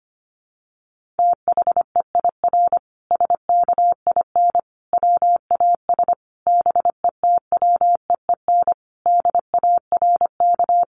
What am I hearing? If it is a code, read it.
Morse code, heard as a keyed tone, about 25 wpm, 700 Hz. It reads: THEIR SKIN WAS BETWEEN DARK
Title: Sentences from Top 1300 Words 25wpm